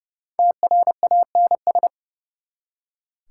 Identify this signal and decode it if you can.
Morse code, heard as a keyed tone, about 39 wpm, 700 Hz. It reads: TRANS